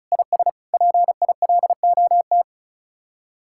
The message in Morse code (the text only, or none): IS PILOT